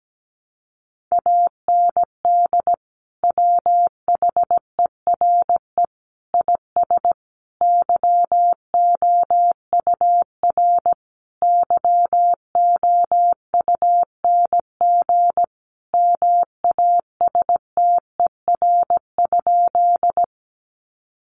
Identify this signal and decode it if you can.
Morse code, heard as a keyed tone, about 17 wpm, 700 Hz. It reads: AND WHERE IS YOUR YOUNG MASTER?
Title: Sentences from Top 1300 Words 17wpm